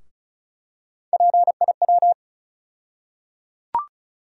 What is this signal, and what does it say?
Morse code, heard as a keyed tone, about 34 wpm, 700 Hz. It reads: PIW
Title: USA Call Sign Postfixes - 3 Letters 35wpm